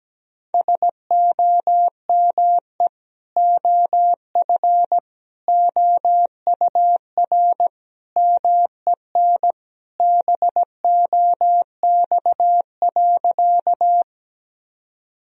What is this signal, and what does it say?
Morse code, heard as a keyed tone, about 17 wpm, 700 Hz. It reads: SOME OF OUR MEN BOX.
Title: Sentences from Top 1000 Words - Review Part 2 - 17wpm